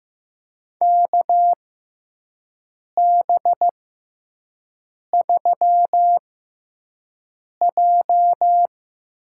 Morse code, heard as a keyed tone, about 15 wpm, spaced as for 5 wpm, 700 Hz. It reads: KB3J
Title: USA Call Signs - 6 Characters - 6x Character Spacing - 15wpm